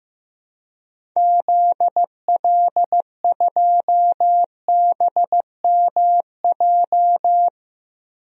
Morse code, heard as a keyed tone, about 15 wpm, 700 Hz. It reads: ZL2BMJ
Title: DXCC-100 Call Signs - 6 Characters - Repeated 3x - 15wpm